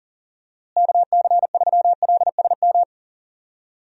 Morse code, heard as a keyed tone, about 37 wpm, 700 Hz. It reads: KC3LSM
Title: USA Call Signs - 6 Characters - Repeated 2x - 40wpm